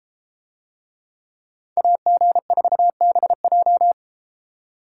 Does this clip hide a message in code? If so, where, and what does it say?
Yes: Morse code from 1.8 to 3.9 s, heard as a keyed tone, about 32 wpm, 700 Hz. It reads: AG4BJ